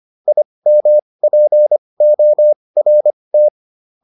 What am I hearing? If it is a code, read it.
Morse code, heard as a keyed tone, about 25 wpm, spaced as for 20 wpm, 600 Hz. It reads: IMPORT